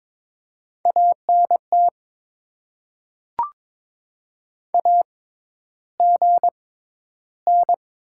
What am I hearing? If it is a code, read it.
Morse code, heard as a keyed tone, about 22 wpm, spaced as for 11 wpm, 700 Hz. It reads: ANT AGN
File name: Top 20 QSO Elements - 6x Character Spacing - 22wpm